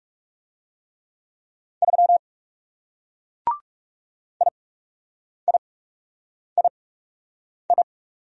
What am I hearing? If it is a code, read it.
Morse code, heard as a keyed tone, about 37 wpm, spaced as for 8 wpm, 700 Hz. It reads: 3 IIII
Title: Speed-Racing Single Letter-Number 30wpm